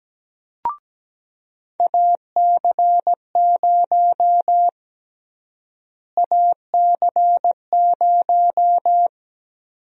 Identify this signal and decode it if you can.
Morse code, heard as a keyed tone, about 17 wpm, 700 Hz. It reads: AC0 AC0
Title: USA Call Sign Prefixes - Repeated 2x - 17wpm